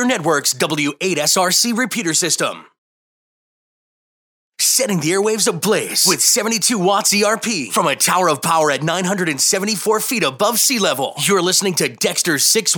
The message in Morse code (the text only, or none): none